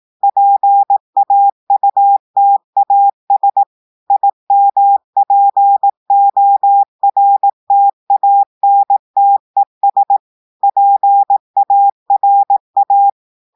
Morse code, heard as a keyed tone, about 18 wpm, 800 Hz. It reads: PAUTAS IMPORTANTES PARA
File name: audios / portugais / portugais texte en clair 2 vitesse 18 mots minutes